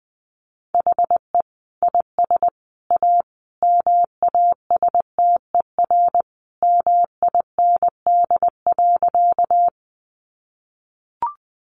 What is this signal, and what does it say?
Morse code, heard as a keyed tone, about 19 wpm, 700 Hz. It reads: HE IS A MASTER MIND.
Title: Sentences from Top 1300 Words 20wpm